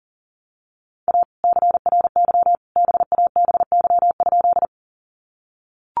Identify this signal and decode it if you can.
Morse code, heard as a keyed tone, about 37 wpm, 700 Hz. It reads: A CRY BABY?
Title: Sentences from Top 1300 Words 40wpm